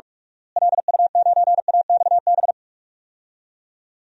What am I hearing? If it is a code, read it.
Morse code, heard as a keyed tone, about 46 wpm, 700 Hz. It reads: LU9AXB
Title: DXCC-100 Call Signs - 6 Characters - Repeated 2x - 45wpm